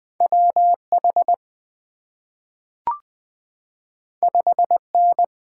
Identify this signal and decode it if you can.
Morse code, heard as a keyed tone, about 20 wpm, 700 Hz. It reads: WH 5N
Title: Two Letter-Number Combinations - 20wpm